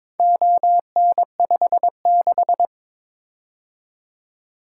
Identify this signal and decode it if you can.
Morse code, heard as a keyed tone, about 22 wpm, 700 Hz. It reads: ON56
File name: practice / 22wpm / Four Letter-Number Combinations - 22wpm